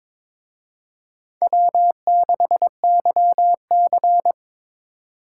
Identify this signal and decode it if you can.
Morse code, heard as a keyed tone, about 26 wpm, 700 Hz. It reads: W6YC
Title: Speed Racing - USA 1x2 Call Signs 22wpm